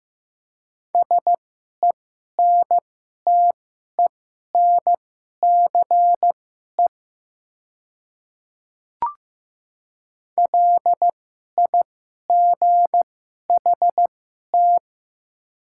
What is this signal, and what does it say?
Morse code, heard as a keyed tone, about 15 wpm, spaced as for 11 wpm, 700 Hz. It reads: SENTENCE LIGHT
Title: 2x Character Spacing - Top 200 Words - 15wpm